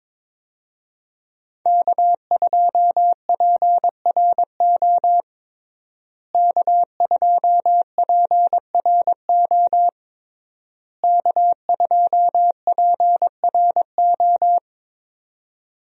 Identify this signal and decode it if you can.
Morse code, heard as a keyed tone, about 21 wpm, 700 Hz. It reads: K2PRO K2PRO K2PRO
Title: USA Call Signs - 5 Characters - Repeated 3x - 22wpm